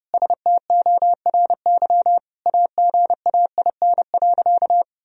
Morse code, heard as a keyed tone, about 30 wpm, 700 Hz. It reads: STORY AGAIN.
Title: Sentences from Top 1000 Words - Review Part 1 - 30wpm